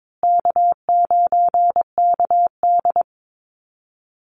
Morse code, heard as a keyed tone, about 22 wpm, 700 Hz. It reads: K9KD
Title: USA Call Signs Repeated 3x 22wpm